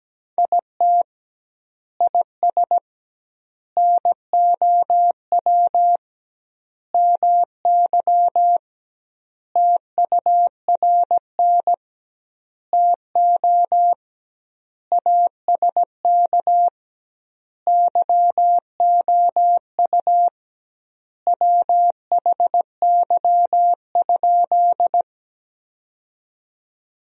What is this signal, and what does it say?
Morse code, heard as a keyed tone, about 17 wpm, 700 Hz. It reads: IT IS NOW MY TURN TO ASK YOU WHY?
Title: 2x Word-Spacing - Sentences from Top 200 Words - 17wpm